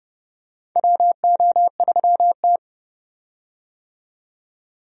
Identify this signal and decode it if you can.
Morse code, heard as a keyed tone, about 31 wpm, 700 Hz. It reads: WO3T
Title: USA Call Signs - 4 Characters - 30wpm